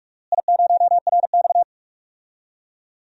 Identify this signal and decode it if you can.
Morse code, heard as a keyed tone, about 41 wpm, 700 Hz. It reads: I0RX
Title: DXCC-100 Call Signs - 4 Characters - Repeated 2x - 45wpm